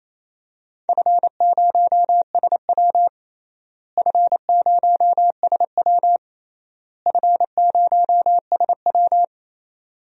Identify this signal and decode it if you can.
Morse code, heard as a keyed tone, about 27 wpm, 700 Hz. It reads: F0SW F0SW F0SW
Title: DXCC-100 Call Signs - 4 Characters - Repeated 3x - 28wpm